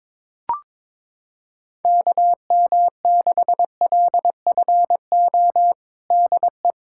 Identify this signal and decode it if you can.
Morse code, heard as a keyed tone, about 22 wpm, 700 Hz. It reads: KM6LFO DE
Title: USA Callsign DE Callsign - 6 Characters - 22wpm